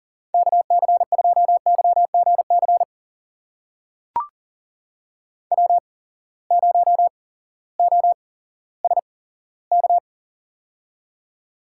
Morse code, heard as a keyed tone, about 40 wpm, spaced as for 18 wpm, 700 Hz. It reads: KC2YGC W0OSK
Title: USA Call Signs - 8x Character Spacing - 40wpm